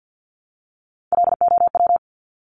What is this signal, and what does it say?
Morse code, heard as a keyed tone, about 50 wpm, 700 Hz. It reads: LOW
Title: Top 200 Words 50wpm